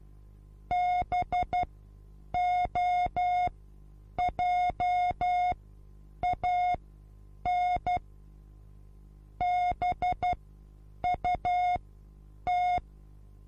Morse code, heard as a keyed tone, about 12 wpm, spaced as for 8 wpm, 720 Hz. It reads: BOJAN BUT
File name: Onlinelehrgang / morsen / DJ4UF Morsen23